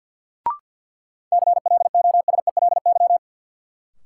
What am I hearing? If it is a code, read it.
Morse code, heard as a keyed tone, about 46 wpm, 700 Hz. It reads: CLOSELY
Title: Sentences from Top 1400 Words - New Word Drill Alt - 50wpm